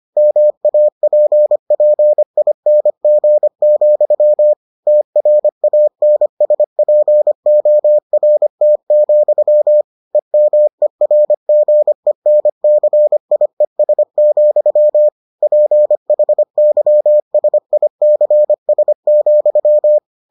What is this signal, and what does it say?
Morse code, heard as a keyed tone, about 25 wpm, 600 Hz. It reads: MAPPING, TRANSPORT, EMERGENCIES, PHYSICS,